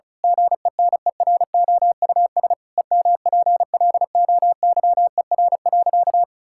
Morse code, heard as a keyed tone, about 36 wpm, 700 Hz. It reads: GENEROUS EMPLOYER.
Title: Pangram Challenge v2 - 35wpm